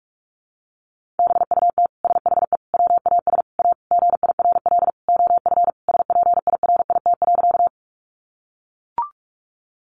Morse code, heard as a keyed tone, about 44 wpm, 700 Hz. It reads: BUT SHE WAS A GIRL OF SPIRIT.
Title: Sentences from Top 1400 Words 45wpm